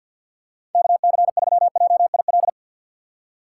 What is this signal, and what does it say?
Morse code, heard as a keyed tone, about 52 wpm, 700 Hz. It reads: KC3JIL
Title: USA Call Signs - 6 Characters - 4x Character Spacing - 50wpm